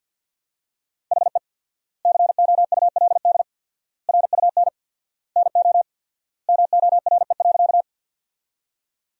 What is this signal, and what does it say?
Morse code, heard as a keyed tone, about 50 wpm, 700 Hz. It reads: HE COULD RUN NO MORE.
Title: 4x Word-Spacing - Sentences from Top 300 Words - 50wpm